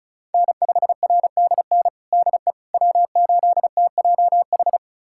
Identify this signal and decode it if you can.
Morse code, heard as a keyed tone, about 36 wpm, 700 Hz. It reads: N5RDN DE W8TJH
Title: USA Callsign DE Callsign - 5 Characters - 35wpm